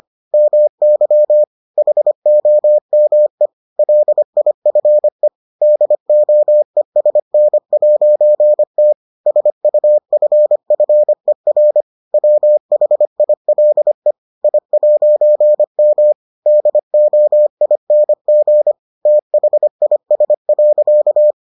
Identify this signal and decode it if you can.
Morse code, heard as a keyed tone, about 25 wpm, 600 Hz. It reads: MY HOME LIFE DOESN'T SUFFER WHILE I'M DOING THIS.